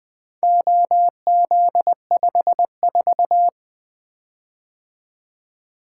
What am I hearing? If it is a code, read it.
Morse code, heard as a keyed tone, about 20 wpm, 700 Hz. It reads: OZ54
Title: Four Letter-Number Combinations - 20wpm